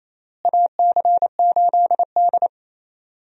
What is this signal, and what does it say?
Morse code, heard as a keyed tone, about 27 wpm, 700 Hz. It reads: AC8D
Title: USA Call Signs - 4 Characters - Repeated 3x - 28wpm